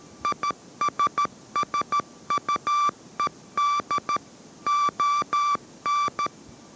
Morse code, heard as a keyed tone, about 13 wpm, 1170 Hz. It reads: ISSUED ON